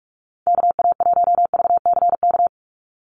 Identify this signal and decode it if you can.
Morse code, heard as a keyed tone, about 44 wpm, 700 Hz. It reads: KA1VCK
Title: USA Call Signs 45wpm